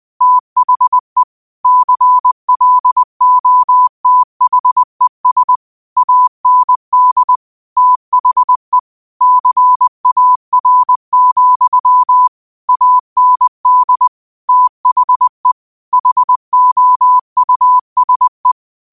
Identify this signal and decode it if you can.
Morse code, heard as a keyed tone, about 20 wpm, 1000 Hz. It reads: THE CLOTHES AND THE CAR, AND THE HOUSE
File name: 20wpm